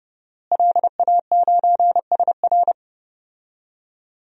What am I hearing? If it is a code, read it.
Morse code, heard as a keyed tone, about 32 wpm, 700 Hz. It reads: LA9SR
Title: DXCC-100 Call Signs - 5 Characters - Repeated 3x - 30wpm